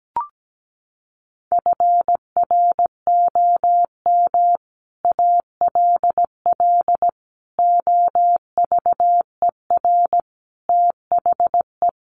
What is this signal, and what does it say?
Morse code, heard as a keyed tone, about 17 wpm, 700 Hz. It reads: FROM ALL OVER THE
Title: Sets of 5 Words Encore 17wpm